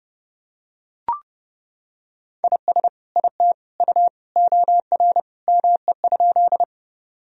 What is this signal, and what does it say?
Morse code, heard as a keyed tone, about 32 wpm, 700 Hz. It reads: IS IT U OR ME?
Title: Sentences from Top 200 Words and Abbreviations 30wpm